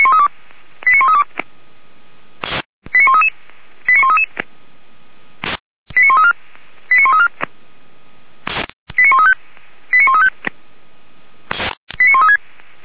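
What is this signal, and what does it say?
Five-tone selective call (ZVEI), heard as quick runs of five tones, tones of about 70 ms. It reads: 89132 89133 89133 89134 89134 89135 89135 89136
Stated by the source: ZVEI five-tone selective call received on SDR.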